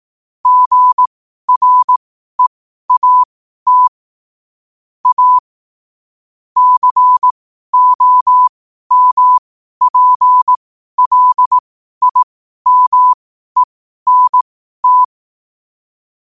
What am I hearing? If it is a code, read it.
Morse code, heard as a keyed tone, about 18 wpm, spaced as for 12 wpm, 1000 Hz. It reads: GREAT A COMPLIMENT